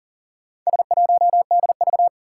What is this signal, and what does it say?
Morse code, heard as a keyed tone, about 40 wpm, 700 Hz. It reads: S1DV
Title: Four Challenging Letter-Number Combinations 40wpm